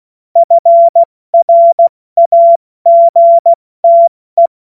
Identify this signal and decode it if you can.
Morse code, heard as a keyed tone, about 16 wpm, spaced as for 14 wpm, 680 Hz. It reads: FRAGTE
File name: Morsekurs / Witze / Witz 23. August 2022 70 F80